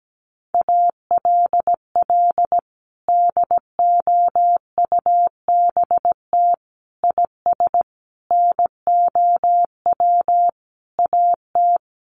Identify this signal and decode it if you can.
Morse code, heard as a keyed tone, about 17 wpm, 700 Hz. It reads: ALL DOUBT IS NOW AT
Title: Sentences from Top 1500 Words 17wpm